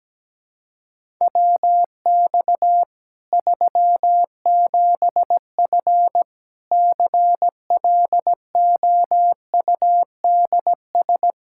Morse code, heard as a keyed tone, about 17 wpm, 700 Hz. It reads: WX 37F CLOUDS
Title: Practice QSO - Exchange 2 - Info - 17wpm